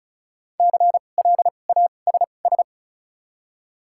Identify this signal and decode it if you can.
Morse code, heard as a keyed tone, about 36 wpm, spaced as for 25 wpm, 700 Hz. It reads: CLASS
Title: 2x Character Spacing - Top 500 Words - 35wpm